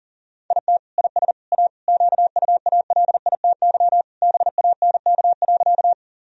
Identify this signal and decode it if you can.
Morse code, heard as a keyed tone, about 37 wpm, 700 Hz. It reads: IT IS A QUALITY BANK.
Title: Sentences from Top 1000 Words - Review Part 4 - 40wpm